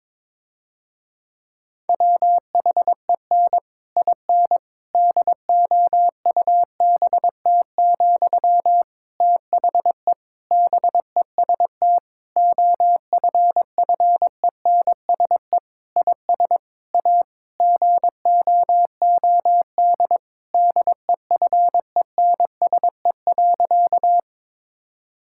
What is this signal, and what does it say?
Morse code, heard as a keyed tone, about 22 wpm, 700 Hz. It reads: WHEN IN DOUBT, THE BEST OFFENSE IS A GOOD DEFENSE.
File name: Difficult Sentences with Hints - 22wpm